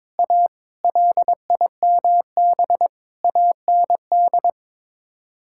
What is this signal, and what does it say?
Morse code, heard as a keyed tone, about 22 wpm, 700 Hz. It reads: A LIMB AND
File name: practice / 22wpm / Sets of 8 Words - 22wpm